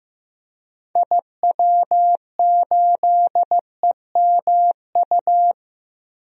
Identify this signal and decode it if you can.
Morse code, heard as a keyed tone, about 15 wpm, 700 Hz. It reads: IW8EMU
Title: DXCC-100 Call Signs - 6 Characters - 15wpm